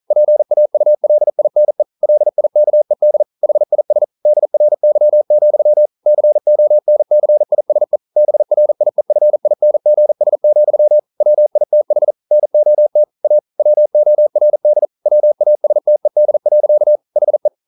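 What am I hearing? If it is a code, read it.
Morse code, heard as a keyed tone, about 42 wpm, 600 Hz. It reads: PAULINE LIKED HIS DRY, CONCISE BRIEFINGS, WITH NOT A WORD WASTED. HE
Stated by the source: Morse code practice at 41 wpm, from CW med Gnister.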